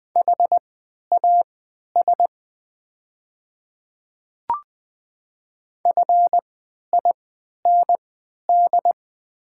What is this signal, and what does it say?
Morse code, heard as a keyed tone, about 20 wpm, spaced as for 11 wpm, 700 Hz. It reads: HAS FIND
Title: 3x Character Spacing - Top 100 Words - 20wpm